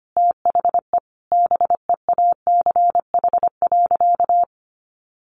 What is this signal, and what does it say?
Morse code, heard as a keyed tone, about 25 wpm, 700 Hz. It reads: THE BEACH.
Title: Sentences from Top 1500 Words 25wpm